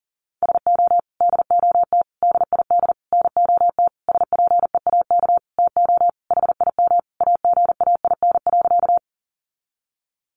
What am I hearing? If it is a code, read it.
Morse code, heard as a keyed tone, about 40 wpm, 700 Hz. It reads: SO DOT DID NOT SPEAK TO HIM AGAIN.
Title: Sentences from Top 1500 Words 40wpm